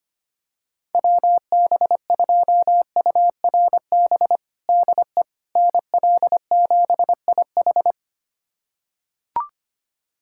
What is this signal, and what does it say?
Morse code, heard as a keyed tone, about 25 wpm, 700 Hz. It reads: WB2URB DE NL7IH
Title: USA Callsign DE Callsign 25wpm